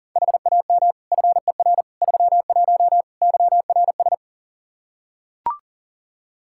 Morse code, heard as a keyed tone, about 37 wpm, 700 Hz. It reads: HAM FER 31 YRS
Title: Practice QSO - Exchange 3 - More Info - 40wpm